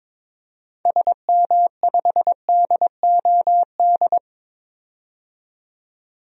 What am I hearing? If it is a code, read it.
Morse code, heard as a keyed tone, about 22 wpm, 700 Hz. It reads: SM5DOD
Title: DXCC-100 Call Signs - 6 Characters - 22wpm